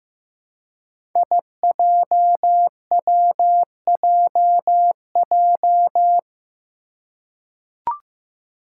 Morse code, heard as a keyed tone, about 15 wpm, 700 Hz. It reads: IJWJJ
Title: Five Challenging Letter-Number Combinations 15wpm